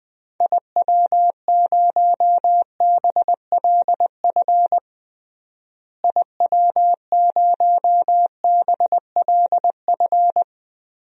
Morse code, heard as a keyed tone, about 21 wpm, 700 Hz. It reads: IW0BLF IW0BLF
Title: DXCC-100 Call Signs - 6 Characters - Repeated 3x - 20wpm